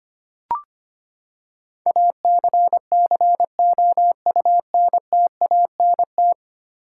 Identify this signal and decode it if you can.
Morse code, heard as a keyed tone, about 25 wpm, 700 Hz. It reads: ACCOUNTANT
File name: practice / 25wpm / Ten Letter Words - 25wpm